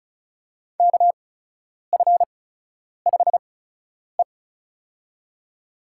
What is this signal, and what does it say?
Morse code, heard as a keyed tone, about 34 wpm, spaced as for 10 wpm, 700 Hz. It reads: KF5E
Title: USA Call Signs - 4 Characters - 8x Character Spacing - 35wpm